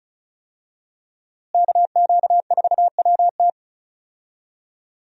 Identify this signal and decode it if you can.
Morse code, heard as a keyed tone, about 34 wpm, 700 Hz. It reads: KQ4WT
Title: USA Call Signs - 5 Characters - Repeated 3x - 35wpm